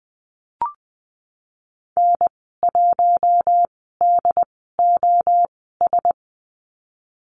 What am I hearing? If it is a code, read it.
Morse code, heard as a keyed tone, about 20 wpm, spaced as for 14 wpm, 700 Hz. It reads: N1DOS
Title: 069 Lesson 12g - Letter d - Call Signs 20wpm